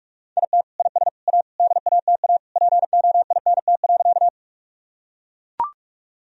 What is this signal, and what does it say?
Morse code, heard as a keyed tone, about 42 wpm, 700 Hz. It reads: IT IS A DATA POINT.
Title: Sentences from Top 1000 Words - Review Part 3 - 45wpm